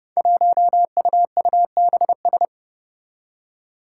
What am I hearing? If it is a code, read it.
Morse code, heard as a keyed tone, about 30 wpm, 700 Hz. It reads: 1UUBS
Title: Five Challenging Letter-Number Combinations 30wpm